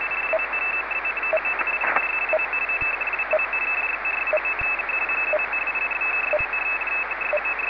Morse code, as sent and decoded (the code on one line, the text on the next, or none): none